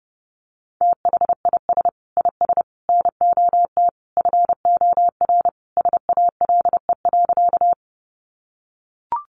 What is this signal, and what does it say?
Morse code, heard as a keyed tone, about 28 wpm, 700 Hz. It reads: THIS IS NOT FOR SALE.
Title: Sentences from Top 1400 Words 30wpm